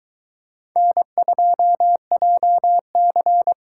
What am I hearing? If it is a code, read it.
Morse code, heard as a keyed tone, about 22 wpm, 700 Hz. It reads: N2JC